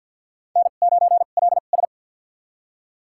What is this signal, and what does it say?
Morse code, heard as a keyed tone, about 47 wpm, spaced as for 35 wpm, 690 Hz. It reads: N9LS
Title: USA Call Signs - 2x Character Spacing - 50wpm